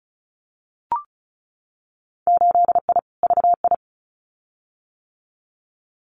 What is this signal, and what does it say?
Morse code, heard as a keyed tone, about 34 wpm, 700 Hz. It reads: 8I VI
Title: Winter Field Day Exchanges Encore 35wpm